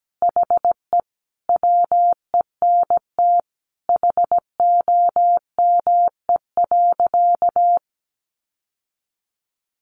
Morse code, heard as a keyed tone, about 17 wpm, 700 Hz. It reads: HE WENT HOME.
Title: Sentences from Top 1200 Words 17wpm